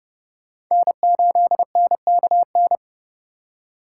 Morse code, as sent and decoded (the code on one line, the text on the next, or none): -. ---.. -. -.- -.
N8NKN